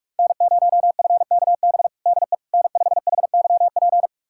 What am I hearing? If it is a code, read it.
Morse code, heard as a keyed tone, about 46 wpm, 700 Hz. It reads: N0FKB DE N5HYP
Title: USA Callsign DE Callsign - 5 Characters - 45wpm